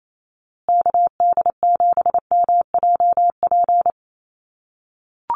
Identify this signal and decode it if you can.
Morse code, heard as a keyed tone, about 28 wpm, 700 Hz. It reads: KD7MJP
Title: USA Call Signs Repeated 3x 28wpm